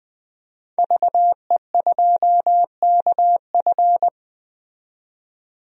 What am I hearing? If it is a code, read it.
Morse code, heard as a keyed tone, about 20 wpm, 700 Hz. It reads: VE2KF
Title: DXCC-100 Call Signs - 5 Characters - 20wpm